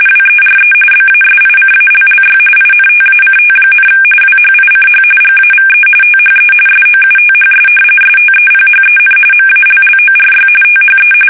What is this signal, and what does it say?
RTTY, about 45.45 baud. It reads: WERE BASICALLY INTACT, WHICH GAVE THE SOUTHS STEPHEN MALLORY THE
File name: fsk850 / usnavy3